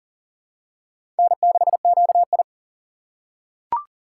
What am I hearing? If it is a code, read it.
Morse code, heard as a keyed tone, about 38 wpm, 700 Hz. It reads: N6QI
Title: USA Call Signs - 6x Character Spacing - 40wpm